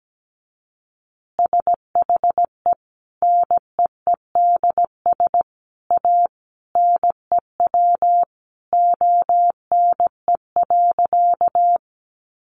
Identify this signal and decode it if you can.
Morse code, heard as a keyed tone, about 17 wpm, 700 Hz. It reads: SHE NEEDS A NEW ONE.
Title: Additional Easy Phrases 17wpm